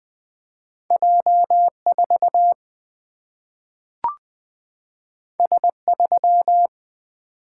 Morse code, heard as a keyed tone, about 20 wpm, 700 Hz. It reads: J 4 S 3
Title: Single-Letter Single-Number 20wpm